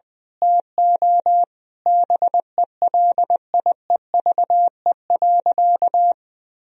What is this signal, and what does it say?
Morse code, heard as a keyed tone, about 20 wpm, 700 Hz. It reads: TO BELIEVE.
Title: Sentences from Top 1000 Words - Review Part 4 - 20wpm